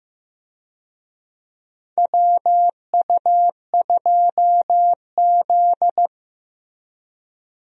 Morse code, heard as a keyed tone, about 16 wpm, 700 Hz. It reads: WU2Z
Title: USA Call Signs - 4 Characters - Repeated 3x - 15wpm